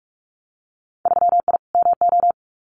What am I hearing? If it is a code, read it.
Morse code, heard as a keyed tone, about 46 wpm, 700 Hz. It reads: 3I MO
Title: Winter Field Day Exchanges 45wpm